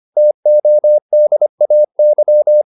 Morse code, heard as a keyed tone, about 25 wpm, 600 Hz. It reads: TODAY
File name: QSO407 25